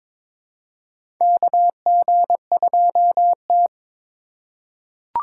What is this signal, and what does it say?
Morse code, heard as a keyed tone, about 22 wpm, 700 Hz. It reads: KG2T
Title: USA Call Signs - 4 Characters - 8x Character Spacing - 22wpm